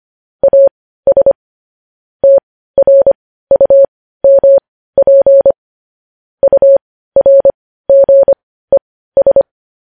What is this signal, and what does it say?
Morse code, heard as a keyed tone, about 25 wpm, spaced as for 15 wpm, 550 Hz. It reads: AS TRUMP URGES
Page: news-headlines-in-morse-code-at-15-wpm